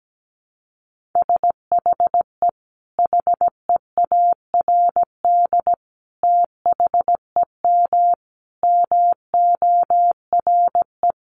Morse code, heard as a keyed tone, about 17 wpm, 700 Hz. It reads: SHE HEARD THEM MORE
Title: Sentences from Top 1300 Words 17wpm